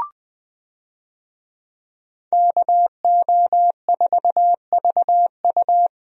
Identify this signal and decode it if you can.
Morse code, heard as a keyed tone, about 20 wpm, 700 Hz. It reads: KO4VU
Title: USA Call Signs - 5 Characters - Repeated 2x - 20wpm